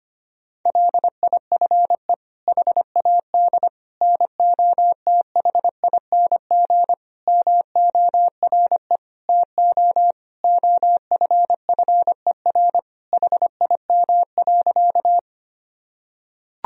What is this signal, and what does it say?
Morse code, heard as a keyed tone, about 25 wpm, 700 Hz. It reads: LIFE HAD NOTHING MORE TO OFFER HIM.
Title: Sentences from Top 1000 Words - Review Part 4 - 25wpm